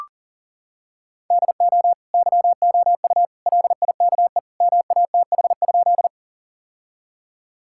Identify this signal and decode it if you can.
Morse code, heard as a keyed tone, about 40 wpm, 700 Hz. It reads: DO YOU LIKE MATH?
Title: Sentences from Top 1000 Words - Review Part 6 - 40wpm